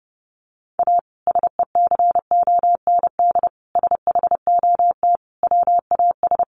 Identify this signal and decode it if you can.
Morse code, heard as a keyed tone, about 30 wpm, 700 Hz. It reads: A SECOND SHOT WAS
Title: Sentences from Top 1100 Words 30wpm